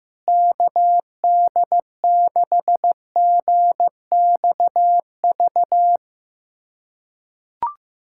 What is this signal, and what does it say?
Morse code, heard as a keyed tone, about 15 wpm, 700 Hz. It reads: KD6GXV
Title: USA Callsign DE Callsign 15wpm